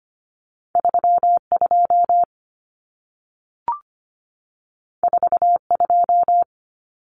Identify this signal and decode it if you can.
Morse code, heard as a keyed tone, about 25 wpm, 700 Hz. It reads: 32 42
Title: Common Cut Numbers - 2-Digit 25wpm